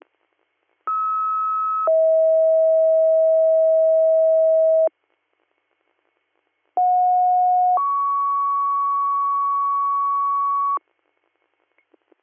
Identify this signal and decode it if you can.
Two-tone page, heard as a one-second tone, then a three-second tone, about 1290 then 640 Hz, about 730 then 1120 Hz. It is heard from 0.9 to 10.8 s.